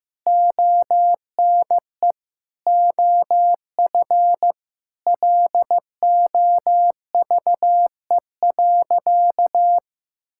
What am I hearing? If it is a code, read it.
Morse code, heard as a keyed tone, about 16 wpm, 700 Hz. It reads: ONE OF LOVE.
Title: Sentences from Top 1000 Words - Review Part 5 - 15wpm